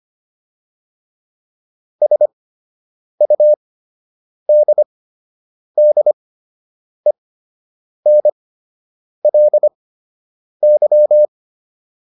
Morse code, heard as a keyed tone, about 25 wpm, spaced as for 8 wpm, 600 Hz. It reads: SUDDENLY